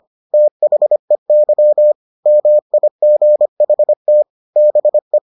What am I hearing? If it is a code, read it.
Morse code, heard as a keyed tone, about 25 wpm, 600 Hz. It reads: THEY MIGHT BE